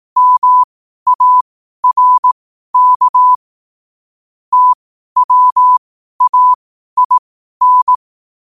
Morse code, heard as a keyed tone, about 18 wpm, spaced as for 12 wpm, 1000 Hz. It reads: MARK TWAIN
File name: qotdfiles / 12wpm